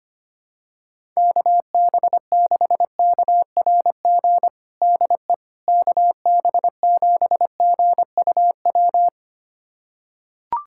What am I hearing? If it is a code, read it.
Morse code, heard as a keyed tone, about 25 wpm, 700 Hz. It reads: KB6KRG DE KB7GUW
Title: USA Callsign DE Callsign - 6 Characters - 25wpm